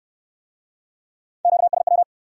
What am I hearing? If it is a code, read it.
Morse code, heard as a keyed tone, about 67 wpm, 700 Hz. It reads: QSW